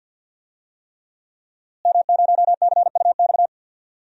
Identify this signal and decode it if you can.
Morse code, heard as a keyed tone, about 50 wpm, 700 Hz. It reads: M0CUX
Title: DXCC-100 Call Signs - 5 Characters - Repeated 3x - 50wpm